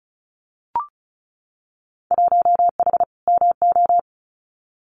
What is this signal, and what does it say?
Morse code, heard as a keyed tone, about 34 wpm, 700 Hz. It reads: 1H MO
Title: Winter Field Day Exchanges 35wpm